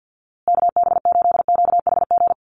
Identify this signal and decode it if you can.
Morse code, heard as a keyed tone, about 50 wpm, 700 Hz. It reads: KB8QHG
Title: USA Call Signs Repeated 2x 50wpm